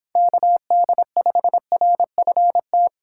Morse code, heard as a keyed tone, about 25 wpm, 700 Hz. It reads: KD5RFT